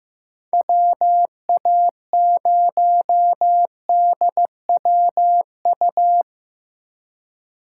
Morse code, heard as a keyed tone, about 15 wpm, 700 Hz. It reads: WA0DWU
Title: USA Call Signs - 6 Characters - 4x Character Spacing - 15wpm